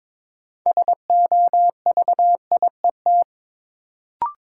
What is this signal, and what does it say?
Morse code, heard as a keyed tone, about 22 wpm, 700 Hz. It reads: SOVIET
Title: Sets of 5 Words Encore - New Word Drill - 22wpm